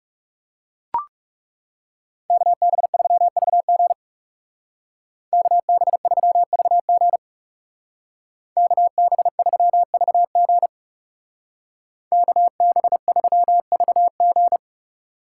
Morse code, heard as a keyed tone, about 36 wpm, 700 Hz. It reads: KB3VG KB3VG KB3VG KB3VG
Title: Speed Racing - USA 2x2 Call Signs 30wpm